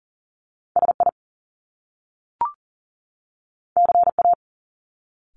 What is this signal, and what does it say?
Morse code, heard as a keyed tone, about 39 wpm, 700 Hz. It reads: SI CA
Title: Top Two Letter Combinations in Words 40wpm